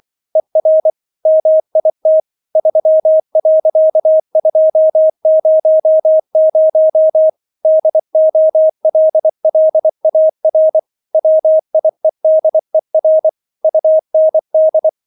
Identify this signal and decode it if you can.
Morse code, heard as a keyed tone, about 24 wpm, 630 Hz. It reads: ER MIT 3.200 DOLLAR WIEDER UND